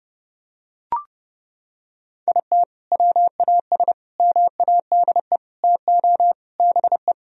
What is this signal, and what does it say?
Morse code, heard as a keyed tone, about 30 wpm, 700 Hz. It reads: IT WAS MADE TO BE
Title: Sentences from Top 1000 Words - Review Part 4 - 30wpm